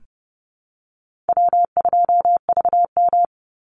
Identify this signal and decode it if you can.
Morse code, heard as a keyed tone, about 30 wpm, 700 Hz. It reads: W2VM
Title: Call Signs - 30wpm